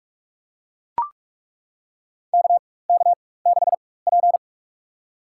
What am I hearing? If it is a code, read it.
Morse code, heard as a keyed tone, about 42 wpm, spaced as for 20 wpm, 700 Hz. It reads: KK6P
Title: USA Call Signs - 4 Characters - 4x Character Spacing - 45wpm